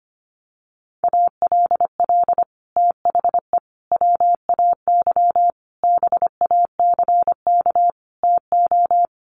Morse code, heard as a keyed tone, about 24 wpm, 700 Hz. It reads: ALL THE WAY BACK TO
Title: Sets of 5 Words Encore 25wpm